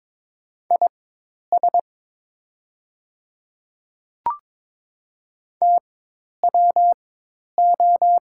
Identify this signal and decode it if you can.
Morse code, heard as a keyed tone, about 22 wpm, spaced as for 10 wpm, 700 Hz. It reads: IS TWO
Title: 4x Character Spacing - Top 100 Words - 22wpm